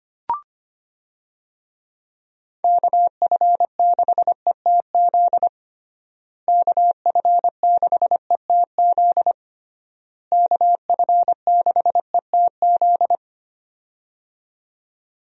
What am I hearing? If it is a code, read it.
Morse code, heard as a keyed tone, about 25 wpm, 700 Hz. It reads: KF6ETZ KF6ETZ KF6ETZ
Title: USA Call Signs - 6 Characters - Repeated 3x - 25wpm